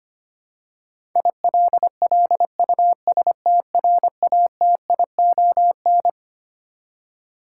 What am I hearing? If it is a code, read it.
Morse code, heard as a keyed tone, about 25 wpm, 700 Hz. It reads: ILLUSTRATION
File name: Twelve Letter Words - 25wpm